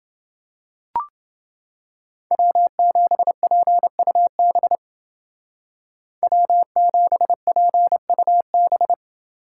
Morse code, heard as a keyed tone, about 28 wpm, 700 Hz. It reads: W7PUB W7PUB